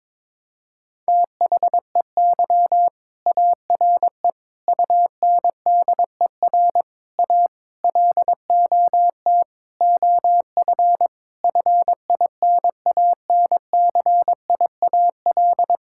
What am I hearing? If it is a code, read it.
Morse code, heard as a keyed tone, about 22 wpm, 700 Hz. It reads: THEY ARE UNDER A LOT OF FINANCIAL
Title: Sentences from Top 1000 Words - Review Part 6 - 22wpm